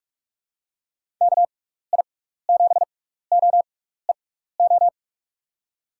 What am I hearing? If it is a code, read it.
Morse code, heard as a keyed tone, about 45 wpm, spaced as for 15 wpm, 700 Hz. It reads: KI7OEO
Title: USA Call Signs - 6 Characters - 6x Character Spacing - 45wpm